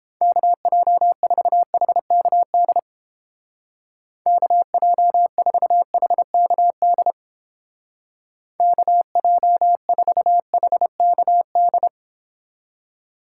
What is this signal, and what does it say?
Morse code, heard as a keyed tone, about 30 wpm, 700 Hz. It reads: KJ4HKD KJ4HKD KJ4HKD